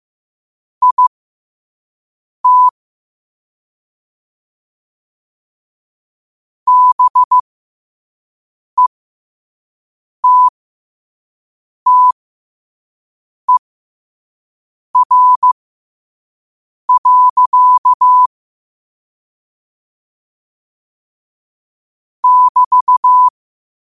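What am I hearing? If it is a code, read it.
Morse code, heard as a keyed tone, about 15 wpm, spaced as for 5 wpm, 1000 Hz. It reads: IT BETTER. <BT>
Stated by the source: Morse code at 5 wpm.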